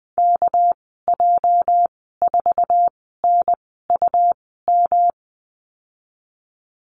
Morse code, heard as a keyed tone, about 20 wpm, spaced as for 14 wpm, 700 Hz. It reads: KJ4NUM
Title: 245 Lesson 39g - Letter z - Call Signs 20wpm